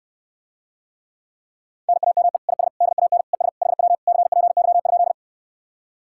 Morse code, heard as a keyed tone, about 69 wpm, 700 Hz. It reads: NAME IS DAN ES HW CPY?